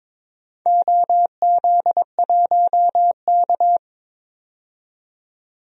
Morse code, heard as a keyed tone, about 22 wpm, 700 Hz. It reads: OZ1K
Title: DXCC-100 Call Signs - 4 Characters - 22wpm